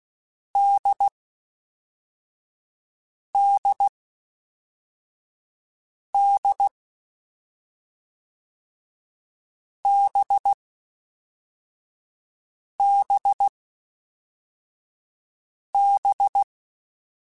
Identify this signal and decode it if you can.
Morse code, heard as a keyed tone, about 16 wpm, 800 Hz. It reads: D D D B B B
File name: Difficult Combos / Difficult Combos-mp3s / NDB NDB6 @5